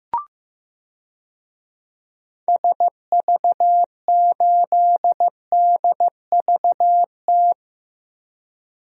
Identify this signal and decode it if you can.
Morse code, heard as a keyed tone, about 15 wpm, 700 Hz. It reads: SV8DVT
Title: DXCC-100 Call Signs - 6 Characters - Repeated 3x - 15wpm